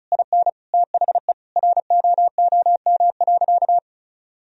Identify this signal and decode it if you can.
Morse code, heard as a keyed tone, about 34 wpm, 700 Hz. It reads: IN THE ROOM.
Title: Sentences from Top 1000 Words - Review Part 4 - 35wpm